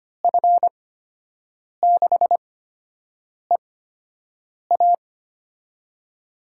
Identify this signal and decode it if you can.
Morse code, heard as a keyed tone, about 25 wpm, spaced as for 7 wpm, 700 Hz. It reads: F6EA
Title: USA Call Signs - 5 Characters - 8x Character Spacing - 25wpm